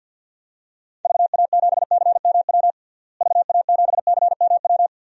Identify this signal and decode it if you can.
Morse code, heard as a keyed tone, about 48 wpm, 700 Hz. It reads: VA7CMW VA7CMW
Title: DXCC-100 Call Signs - 6 Characters - Repeated 3x - 50wpm